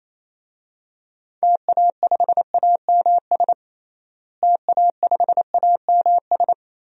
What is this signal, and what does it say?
Morse code, heard as a keyed tone, about 28 wpm, 700 Hz. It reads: TA5AMS TA5AMS
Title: DXCC-100 Call Signs - 6 Characters - Repeated 3x - 28wpm